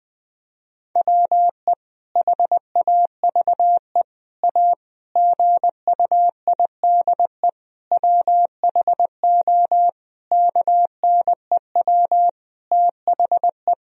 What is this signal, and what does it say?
Morse code, heard as a keyed tone, about 20 wpm, 700 Hz. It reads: WE HAVE A GUIDE WHO KNEW THE
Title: Sentences from Top 1000 Words - Review Part 3 - 20wpm